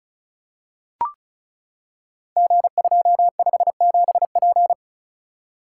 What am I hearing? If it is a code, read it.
Morse code, heard as a keyed tone, about 36 wpm, 690 Hz. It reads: G257P
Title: Code Groups - 5 Letter:Numbers - 35wpm